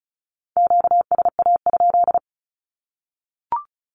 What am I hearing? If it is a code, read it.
Morse code, heard as a keyed tone, about 34 wpm, 700 Hz. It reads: QSA?
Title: Q-Codes Repeated 3x 35wpm